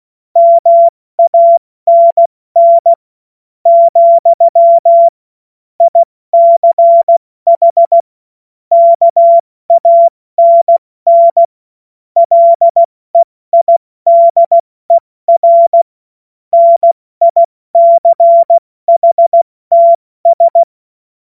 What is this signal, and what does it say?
Morse code, heard as a keyed tone, about 16 wpm, spaced as for 14 wpm, 680 Hz. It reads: MANN, ICH KANN LEIDER NICHTS
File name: Morsekurs / Witze / Witz 13. Dezember 2022 70 F80